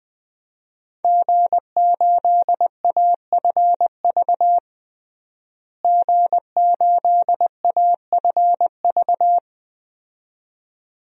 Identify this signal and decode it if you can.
Morse code, heard as a keyed tone, about 21 wpm, 700 Hz. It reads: G8AFV G8AFV
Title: DXCC-100 Call Signs - 5 Characters - Repeated 2x - 20wpm